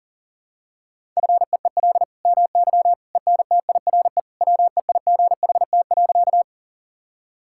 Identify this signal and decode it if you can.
Morse code, heard as a keyed tone, about 37 wpm, 700 Hz. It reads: FEEL MY ENTIRE WEIGHT.
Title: Sentences from Top 1000 Words - Review Part 5 - 40wpm